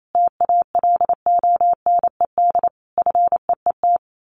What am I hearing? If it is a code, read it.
Morse code, heard as a keyed tone, about 28 wpm, 700 Hz. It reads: TALONED FEET
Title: A Sound Of Thunder 28wpm